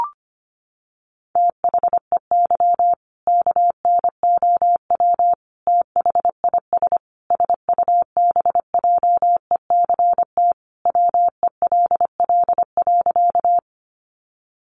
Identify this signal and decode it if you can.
Morse code, heard as a keyed tone, about 25 wpm, 700 Hz. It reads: THEY KNOW THIS SUBJECT WELL.
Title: Sentences from Top 800 Words 25wpm